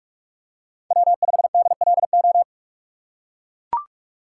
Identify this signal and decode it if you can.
Morse code, heard as a keyed tone, about 44 wpm, 700 Hz. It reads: W5DLO
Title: USA Call Signs - 5 Characters - 6x Character Spacing - 45wpm